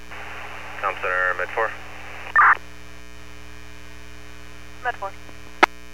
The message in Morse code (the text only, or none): none